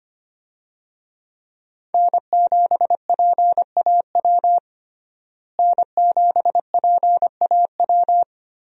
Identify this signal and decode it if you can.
Morse code, heard as a keyed tone, about 25 wpm, 700 Hz. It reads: N7PAW N7PAW
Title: USA Call Signs - 5 Characters - Repeated 3x - 25wpm